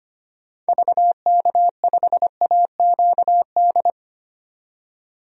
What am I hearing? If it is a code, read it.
Morse code, heard as a keyed tone, about 25 wpm, 700 Hz. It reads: VK5AQD
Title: DXCC-100 Call Signs - 6 Characters - 25wpm